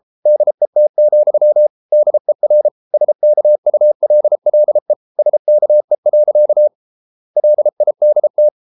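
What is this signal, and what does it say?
Morse code, heard as a keyed tone, about 33 wpm, 600 Hz. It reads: DET, DER SKULLE SKE. LIDT
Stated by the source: Morse code at 33 wpm, QRQ.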